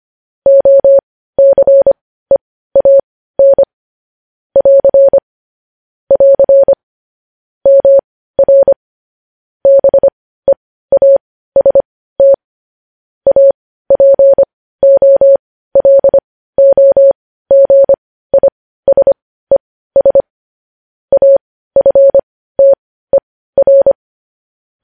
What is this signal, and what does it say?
Morse code, heard as a keyed tone, about 25 wpm, spaced as for 15 wpm, 550 Hz. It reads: OCEAN <AR> <AR> MR BEAST APOLOGISES AFTER
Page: news headlines in morse code at 15 wpm